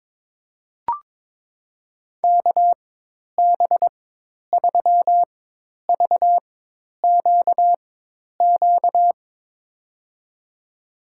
Morse code, heard as a keyed tone, about 22 wpm, spaced as for 10 wpm, 700 Hz. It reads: KB3VQQ
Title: USA Call Signs - 6 Characters - 4x Character Spacing - 22wpm